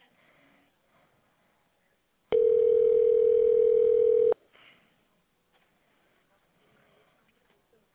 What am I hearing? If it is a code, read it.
Ringback tone (North America), 440 + 480 Hz.